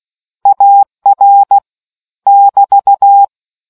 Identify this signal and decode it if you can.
Morse code, heard as a keyed tone, about 16 wpm, 800 Hz. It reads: AR <BT>